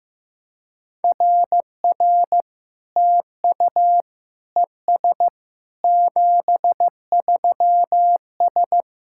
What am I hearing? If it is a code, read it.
Morse code, heard as a keyed tone, about 15 wpm, 700 Hz. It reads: RR TU ES 73S
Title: SOTA QSO Practice 15wpm